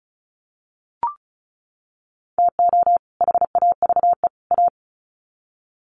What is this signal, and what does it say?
Morse code, heard as a keyed tone, about 34 wpm, 700 Hz. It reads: TO HAVE A
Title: Sets of 3 Words 35wpm